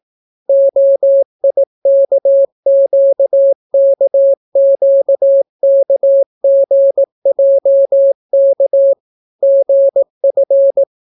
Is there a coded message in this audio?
Yes: Morse code from 0.5 to 10.8 s, heard as a keyed tone, about 18 wpm, 550 Hz. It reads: OIKQKQKGJK GF